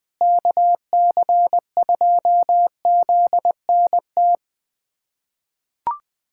Morse code, heard as a keyed tone, about 20 wpm, 700 Hz. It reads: KC2ZNT
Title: USA Callsign DE Callsign 20wpm